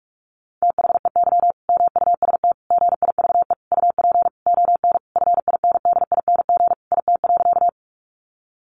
Morse code, heard as a keyed tone, about 44 wpm, 700 Hz. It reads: THEY MUST GIVE UP ON FINDING IT.
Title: Sentences from Top 1100 Words 45wpm